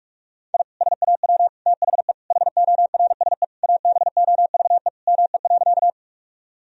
Morse code, heard as a keyed tone, about 46 wpm, 700 Hz. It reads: I SAW THE HORSE ABOVE ME.
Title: Sentences from Top 1000 Words - Review Part 2 - 45wpm